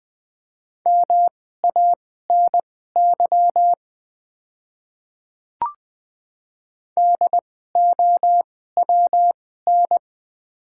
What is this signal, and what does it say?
Morse code, heard as a keyed tone, about 20 wpm, spaced as for 14 wpm, 700 Hz. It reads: MANY DOWN
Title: 2x Character Spacing - Top 100 Words - 20wpm